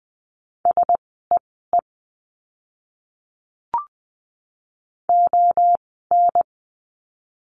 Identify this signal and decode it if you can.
Morse code, heard as a keyed tone, about 20 wpm, spaced as for 15 wpm, 700 Hz. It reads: SEE ON
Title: 146 Lesson 24f - Letter y - Words 20wpm